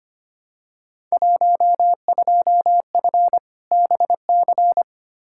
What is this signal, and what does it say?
Morse code, heard as a keyed tone, about 25 wpm, 700 Hz. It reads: 12F BC
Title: Field Day Exchanges Encore - 25wpm